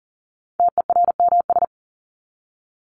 Morse code, heard as a keyed tone, about 40 wpm, 700 Hz. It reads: TERMS
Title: ES:S Suffix - Top 100 Words with Base Words 40wpm